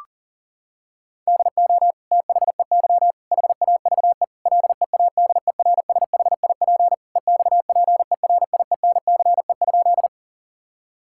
Morse code, heard as a keyed tone, about 40 wpm, 700 Hz. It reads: DO THEY HAVE LEADERSHIP EXPERIENCE?
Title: Sentences from Top 1600 Words 40wpm